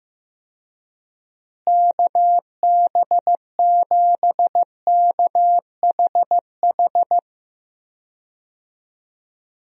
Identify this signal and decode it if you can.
Morse code, heard as a keyed tone, about 16 wpm, 700 Hz. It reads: KB7KHH
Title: USA Call Signs - 6 Characters - Repeated 2x - 15wpm